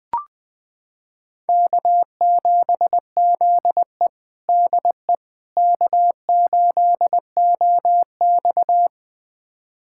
Morse code, heard as a keyed tone, about 20 wpm, 700 Hz. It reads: K7ZE DE K8OX
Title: USA Callsign DE Callsign - 4 Characters - 20wpm